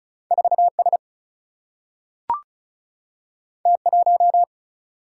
Morse code, heard as a keyed tone, about 36 wpm, 700 Hz. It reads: 4 S T 1
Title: Single-Letter Single-Number 35wpm